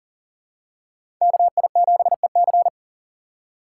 Morse code, heard as a keyed tone, about 40 wpm, 700 Hz. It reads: KI7EC
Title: USA Call Signs - 3x Character Spacing - 40wpm